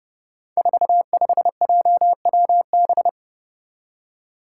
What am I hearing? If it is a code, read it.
Morse code, heard as a keyed tone, about 30 wpm, 700 Hz. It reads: 45JWB
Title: Five Challenging Letter-Number Combinations 30wpm